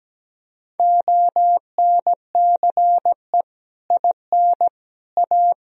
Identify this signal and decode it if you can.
Morse code, heard as a keyed tone, about 17 wpm, 700 Hz. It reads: ONCE IN A
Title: Speed-Racing - Sets of 4 Words 17wpm